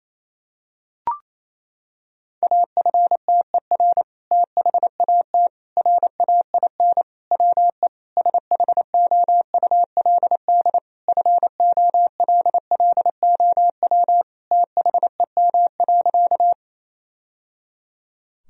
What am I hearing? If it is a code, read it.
Morse code, heard as a keyed tone, about 28 wpm, 700 Hz. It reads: AFTER THAT RAIN WE SHOULD FOLLOW THEM.
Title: Sentences from Top 1000 Words - Review Part 1 - 28wpm